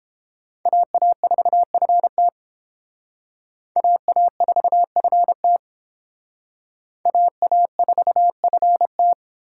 Morse code, heard as a keyed tone, about 30 wpm, 700 Hz. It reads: AA4FT AA4FT AA4FT